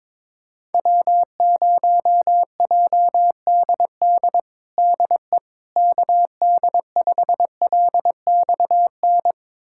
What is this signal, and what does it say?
Morse code, heard as a keyed tone, about 22 wpm, 700 Hz. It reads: W0JDD DE KD5LXN
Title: USA Callsign DE Callsign 22wpm